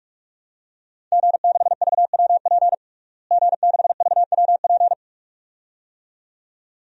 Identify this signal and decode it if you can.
Morse code, heard as a keyed tone, about 44 wpm, 700 Hz. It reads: G6VWP G6VWP
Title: DXCC-100 Call Signs - 5 Characters - Repeated 2x - 45wpm